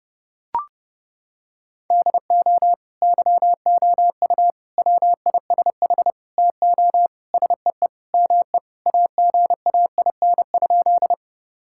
Morse code, heard as a keyed tone, about 30 wpm, 700 Hz. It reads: DO YOU WISH TO SEE ME AGAIN?
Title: Sentences from Top 1600 Words 30wpm